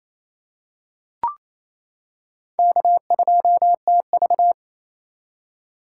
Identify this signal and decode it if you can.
Morse code, heard as a keyed tone, about 27 wpm, 700 Hz. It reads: K2TV
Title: K1USN SST QSO Practice - 28wpm